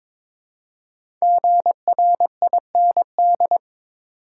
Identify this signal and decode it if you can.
Morse code, heard as a keyed tone, about 22 wpm, 690 Hz. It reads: GRIND
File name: 5 Letter Words - First 3 Letters Given - 22wpm